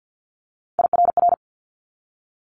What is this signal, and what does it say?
Morse code, heard as a keyed tone, about 50 wpm, 700 Hz. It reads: IRR
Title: Multi-letter Phonemes 50wpm